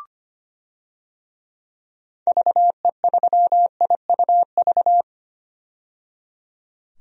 Morse code, heard as a keyed tone, about 25 wpm, 700 Hz. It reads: VE3IUV
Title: DXCC-100 Call Signs - 6 Characters - 25wpm